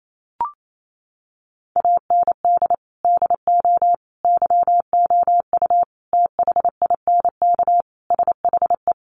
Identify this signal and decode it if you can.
Morse code, heard as a keyed tone, about 28 wpm, 700 Hz. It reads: AND DO YOU THINK SHE
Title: Sentences from Top 600 Words 28wpm